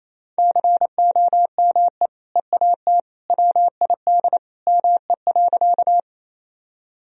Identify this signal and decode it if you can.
Morse code, heard as a keyed tone, about 28 wpm, 700 Hz. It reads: COME EAT WID ME.
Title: Sentences from Top 300 Words and Abbreviations 28wpm